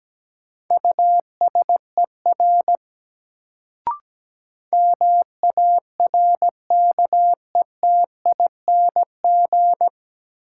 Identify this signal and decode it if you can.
Morse code, heard as a keyed tone, about 17 wpm, 700 Hz. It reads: USER MARKETING
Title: Sentences from Top 1000 Words - Review Part 5 - New Word Drill - 17wpm